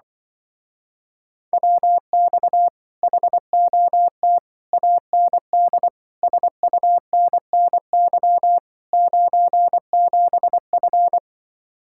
Morse code, heard as a keyed tone, about 24 wpm, 700 Hz. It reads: WX HOT AND SUNNY 97F
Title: Speed-Racing Common QSO Phrases Encore - 20wpm